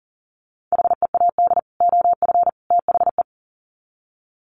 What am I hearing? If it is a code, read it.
Morse code, heard as a keyed tone, about 40 wpm, 700 Hz. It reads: HEAD OF THE
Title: Sets of 3 Words 40wpm